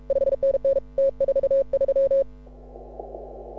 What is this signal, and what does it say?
Morse code, heard as a keyed tone, about 36 wpm, 530 Hz. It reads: 5NN T43
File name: AUDIO2018 / HA6OI CW